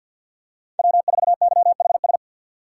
Morse code, heard as a keyed tone, about 49 wpm, 700 Hz. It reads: W4YHS
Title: USA Call Signs - 2x Character Spacing - 50wpm